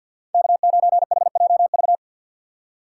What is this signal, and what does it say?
Morse code, heard as a keyed tone, about 63 wpm, 700 Hz. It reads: K9HJV